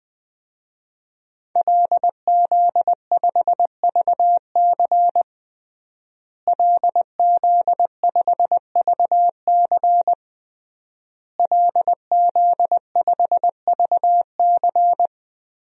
Morse code, heard as a keyed tone, about 20 wpm, 700 Hz. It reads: LZ5VC LZ5VC LZ5VC
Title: DXCC-100 Call Signs - 5 Characters - Repeated 3x - 20wpm